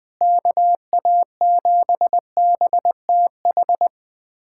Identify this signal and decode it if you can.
Morse code, heard as a keyed tone, about 20 wpm, 700 Hz. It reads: KA7BTH
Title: USA Callsign DE Callsign 20wpm